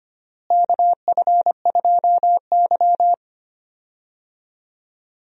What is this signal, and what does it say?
Morse code, heard as a keyed tone, about 25 wpm, 700 Hz. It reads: KF2Y
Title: USA Call Signs - 4 Characters - Repeated 2x - 25wpm